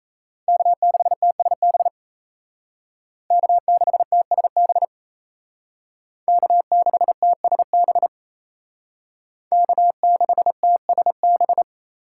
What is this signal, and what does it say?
Morse code, heard as a keyed tone, about 35 wpm, 700 Hz. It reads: K6TSB K6TSB K6TSB K6TSB
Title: Speed Racing - USA 1x3 Call Signs 28wpm